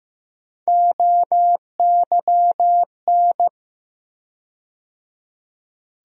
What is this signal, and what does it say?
Morse code, heard as a keyed tone, about 15 wpm, 700 Hz. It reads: OYN
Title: USA Call Sign Postfixes - 3 Letters 15wpm